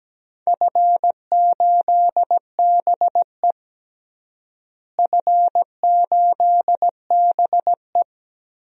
Morse code, heard as a keyed tone, about 17 wpm, 700 Hz. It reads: F8BE F8BE
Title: DXCC-100 Call Signs - 4 Characters - Repeated 2x - 17wpm